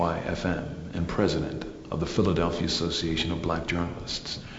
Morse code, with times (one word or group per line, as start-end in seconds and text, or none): none